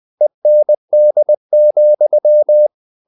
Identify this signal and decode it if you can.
Morse code, heard as a keyed tone, about 20 wpm, 600 Hz. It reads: END,
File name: the old man and the sea part 2 20wpm